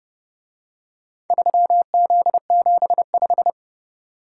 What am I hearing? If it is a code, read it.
Morse code, heard as a keyed tone, about 28 wpm, 700 Hz. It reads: 3Z75